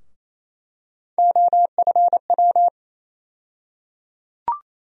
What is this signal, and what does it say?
Morse code, heard as a keyed tone, about 28 wpm, 700 Hz. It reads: OFW
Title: USA Call Sign Postfixes - 3 Letters 28wpm